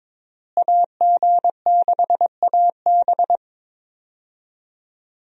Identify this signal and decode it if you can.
Morse code, heard as a keyed tone, about 22 wpm, 700 Hz. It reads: AG6AB
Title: USA Call Signs - 5 Characters - Repeated 3x - 22wpm